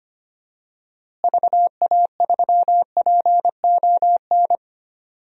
Morse code, heard as a keyed tone, about 25 wpm, 700 Hz. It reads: VA3PON
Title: DXCC-100 Call Signs - 6 Characters - Repeated 3x - 25wpm